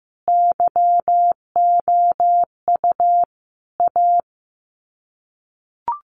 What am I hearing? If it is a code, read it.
Morse code, heard as a keyed tone, about 15 wpm, 700 Hz. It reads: YOU A
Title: Sets of 5 Words Encore 15wpm